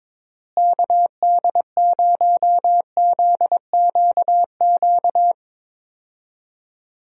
Morse code, heard as a keyed tone, about 22 wpm, 700 Hz. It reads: KD0ZQQ
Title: USA Call Signs - 6 Characters - Repeated 2x - 22wpm